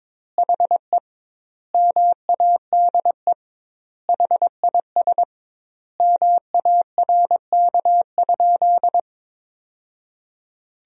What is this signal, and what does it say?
Morse code, heard as a keyed tone, about 22 wpm, 700 Hz. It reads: HE MADE HIS MARK?
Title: 2x Word-Spacing - Sentences from Top 300 Words - 22wpm